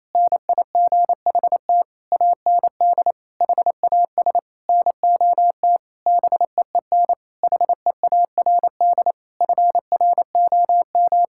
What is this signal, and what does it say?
Morse code, heard as a keyed tone, about 28 wpm, 700 Hz. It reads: NIGHT AND HAS NOT BEEN HEARD FROM
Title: Sentences from Top 1600 Words 28wpm